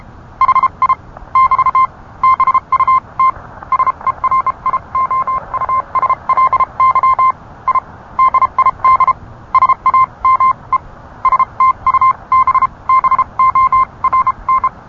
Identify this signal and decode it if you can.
Morse code, heard as a keyed tone, about 31 wpm, spaced as for 26 wpm, 1050 Hz. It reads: HI <BT> BUT SERIOUSLY I DID SAME STUBBORN